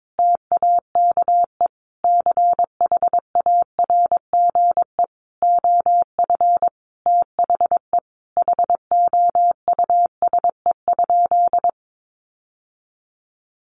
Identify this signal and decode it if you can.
Morse code, heard as a keyed tone, about 22 wpm, 700 Hz. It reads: TAKE CHARGE OF THE HOUSE?
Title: Sentences from Top 900 Words 22wpm